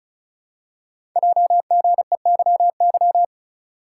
Morse code, heard as a keyed tone, about 36 wpm, 690 Hz. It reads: JGEYY
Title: Code Groups - 5 Letters - 35wpm